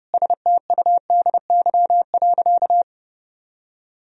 Morse code, heard as a keyed tone, about 30 wpm, 700 Hz. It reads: STUDY.
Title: Sentences from Top 1000 Words - Review Part 1 - 30wpm